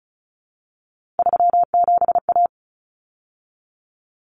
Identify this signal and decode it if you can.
Morse code, heard as a keyed tone, about 34 wpm, 700 Hz. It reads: 37A
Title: Common Cut Numbers - 3-Digit 35wpm